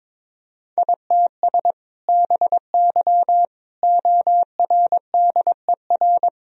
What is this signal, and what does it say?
Morse code, heard as a keyed tone, about 22 wpm, 700 Hz. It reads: ITS BY ORDER
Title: Sentences from Top 1000 Words - Review Part 3 - 22wpm